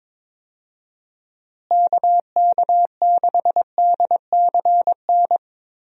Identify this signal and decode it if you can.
Morse code, heard as a keyed tone, about 22 wpm, 700 Hz. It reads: KK6DCN
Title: USA Call Signs - 6 Characters - Repeated 3x - 22wpm